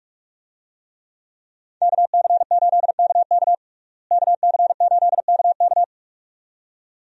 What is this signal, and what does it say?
Morse code, heard as a keyed tone, about 42 wpm, 700 Hz. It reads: KC8KK KC8KK
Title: USA Call Signs - 5 Characters - Repeated 2x - 45wpm